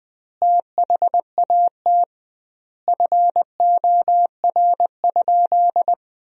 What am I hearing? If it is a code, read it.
Morse code, heard as a keyed tone, about 20 wpm, 700 Hz. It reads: THAT FOR?
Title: 2x Word-Spacing - Sentences from Top 200 Words - 20wpm